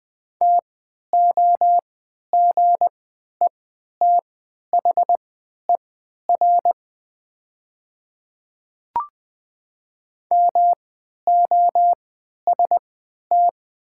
Morse code, heard as a keyed tone, about 20 wpm, spaced as for 11 wpm, 700 Hz. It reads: TOGETHER MOST
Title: 3x Character Spacing - Top 300 Words - 20wpm